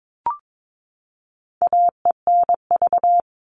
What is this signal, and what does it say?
Morse code, heard as a keyed tone, about 22 wpm, 700 Hz. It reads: AENV
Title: All Cut Numbers - 4-Digit 22wpm